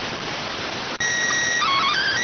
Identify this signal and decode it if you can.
SSTV (slow-scan television), PD 290.